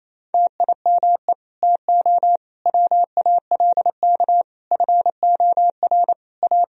Morse code, heard as a keyed tone, about 28 wpm, 700 Hz. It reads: TIME TO WALK FOR A
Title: Sentences from Top 1000 Words - Review Part 1 - 28wpm